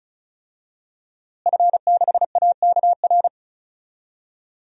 Morse code, heard as a keyed tone, about 36 wpm, 700 Hz. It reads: F6AKR
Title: DXCC-100 Call Signs - 5 Characters - Repeated 2x - 35wpm